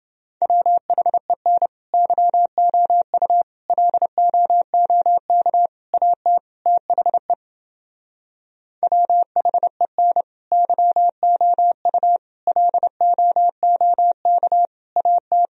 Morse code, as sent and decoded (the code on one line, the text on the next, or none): .-- .... -. / -.-- --- ..- / .-.. --- --- -.- / .- - / - .... / .-- .... -. / -.-- --- ..- / .-.. --- --- -.- / .- -
WHEN YOU LOOK AT THE WHEN YOU LOOK AT